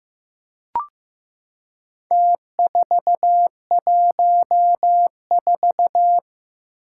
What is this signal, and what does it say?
Morse code, heard as a keyed tone, about 15 wpm, 700 Hz. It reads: T414
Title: Four Challenging Letter-Number Combinations 15wpm